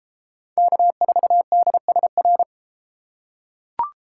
Morse code, heard as a keyed tone, about 32 wpm, 700 Hz. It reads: K4DSR